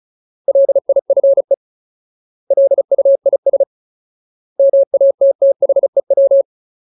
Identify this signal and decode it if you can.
Morse code, heard as a keyed tone, about 34 wpm, 550 Hz. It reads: LIFE LUIS MATTHEW
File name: Ultimate Word List / 35WPM Ultimate